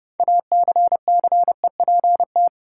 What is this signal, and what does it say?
Morse code, heard as a keyed tone, about 30 wpm, 700 Hz. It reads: ACCEPT
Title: Sentences from Top 1300 Words - New Word Drill - 30wpm